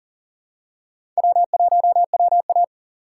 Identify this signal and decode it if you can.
Morse code, heard as a keyed tone, about 40 wpm, 700 Hz. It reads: W1WA
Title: USA Call Signs - 4 Characters - 2x Character Spacing - 40wpm